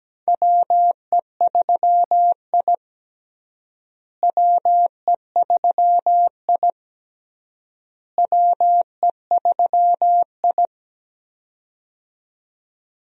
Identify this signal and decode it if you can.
Morse code, heard as a keyed tone, about 17 wpm, 700 Hz. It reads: WE3I WE3I WE3I
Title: USA Call Signs - 4 Characters - Repeated 3x - 17wpm